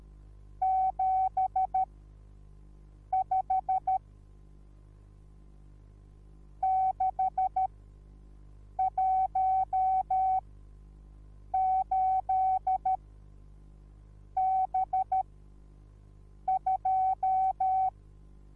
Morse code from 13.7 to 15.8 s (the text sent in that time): B